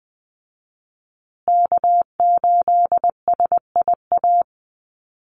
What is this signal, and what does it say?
Morse code, heard as a keyed tone, about 20 wpm, 700 Hz. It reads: K8SIA
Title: Call Signs Repeated 2x - 20wpm